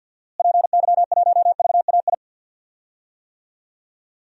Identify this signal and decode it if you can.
Morse code, heard as a keyed tone, about 52 wpm, 700 Hz. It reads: PY1VAI
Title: DXCC-100 Call Signs - 6 Characters - Repeated 2x - 50wpm